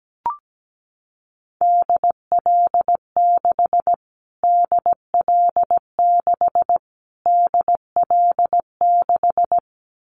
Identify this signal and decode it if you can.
Morse code, heard as a keyed tone, about 17 wpm, 700 Hz. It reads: DL6 DL6 DL6
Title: Callsign Prefixes for Top 25 DXCC - Repeated 3x - 17wpm